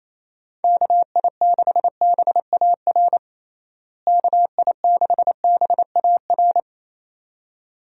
Morse code, heard as a keyed tone, about 28 wpm, 700 Hz. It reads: KI6BAR KI6BAR
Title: USA Call Signs - 6 Characters - Repeated 2x - 28wpm